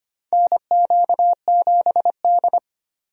Morse code, heard as a keyed tone, about 25 wpm, 700 Hz. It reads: NQ7D
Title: USA Call Signs - 4 Characters - 25wpm